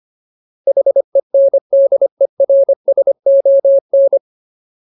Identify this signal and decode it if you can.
Morse code, heard as a keyed tone, about 25 wpm, 550 Hz. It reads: HENDERSON
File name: Ultimate Word List / 25WPM Ultimate